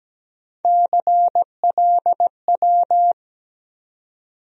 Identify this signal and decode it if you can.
Morse code, heard as a keyed tone, about 17 wpm, 700 Hz. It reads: CLW
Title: USA Call Sign Postfixes - 3 Letters 17wpm